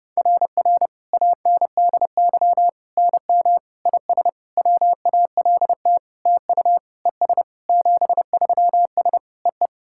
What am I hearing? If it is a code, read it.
Morse code, heard as a keyed tone, about 30 wpm, 700 Hz. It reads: RR ANDY NM IS WALT TU ES 73S EE
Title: SOTA QSO Practice 30wpm